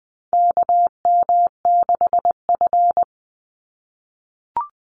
Morse code, heard as a keyed tone, about 20 wpm, 700 Hz. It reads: KM6F
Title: USA Call Signs 20wpm